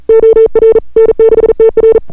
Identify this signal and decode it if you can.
Morse code, heard as a keyed tone, about 38 wpm, 440 Hz. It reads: OR N6TR
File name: n6tr / Fab2007CWSprintHour1